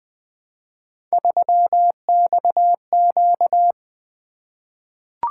Morse code, heard as a keyed tone, about 20 wpm, 700 Hz. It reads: 3XQ
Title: Three Letter-Number Combinations - 20wpm